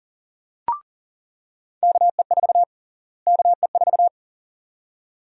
Morse code, heard as a keyed tone, about 40 wpm, 700 Hz. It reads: KE4 KE4
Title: USA Call Sign Prefixes - Repeated 2x - 40wpm